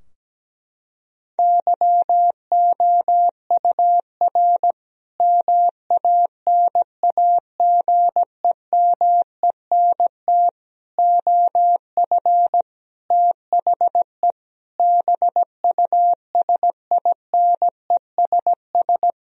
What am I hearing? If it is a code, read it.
Morse code, heard as a keyed tone, about 17 wpm, 700 Hz. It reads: YOUR MANAGEMENT OF THE BUSINESS
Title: Sentences from Top 1000 Words - Review Part 4 - 17wpm